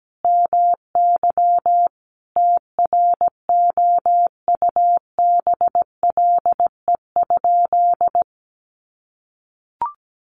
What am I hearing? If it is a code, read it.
Morse code, heard as a keyed tone, about 17 wpm, 700 Hz. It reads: MY TROUBLE?
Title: Sentences from Top 1200 Words 17wpm